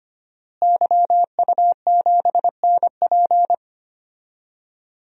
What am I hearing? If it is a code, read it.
Morse code, heard as a keyed tone, about 25 wpm, 700 Hz. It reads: YU7NP
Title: DXCC-100 Call Signs - 5 Characters - Repeated 2x - 25wpm